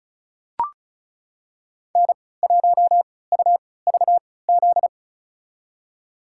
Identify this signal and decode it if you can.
Morse code, heard as a keyed tone, about 36 wpm, spaced as for 20 wpm, 700 Hz. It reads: N1UVZ
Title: USA Call Signs - 3x Character Spacing - 35wpm